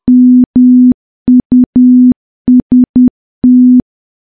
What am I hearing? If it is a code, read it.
Morse code, heard as a keyed tone, about 10 wpm, 250 Hz. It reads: MUST
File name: morse / morse-training-10WPM